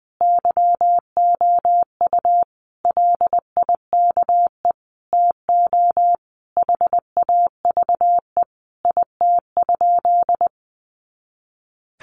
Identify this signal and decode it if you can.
Morse code, heard as a keyed tone, about 20 wpm, 700 Hz. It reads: YOU LIKE TO HAVE IT?
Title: Sentences from Top 100 Words 20wpm